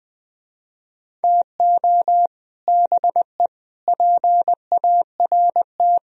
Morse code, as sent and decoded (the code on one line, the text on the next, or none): - --- / -... / .--. .- .-. -
TO BE PART